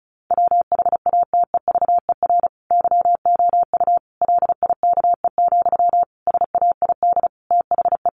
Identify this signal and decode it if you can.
Morse code, heard as a keyed tone, about 34 wpm, 700 Hz. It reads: WHATEVER YOU LIKE, SAID THE
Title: Sentences from Top 1100 Words 35wpm